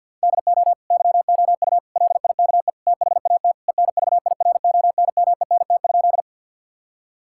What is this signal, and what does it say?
Morse code, heard as a keyed tone, about 52 wpm, 700 Hz. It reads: DO YOU LIKE THAT ENVIRONMENT?
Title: Sentences from Top 1000 Words - Review Part 5 - 50wpm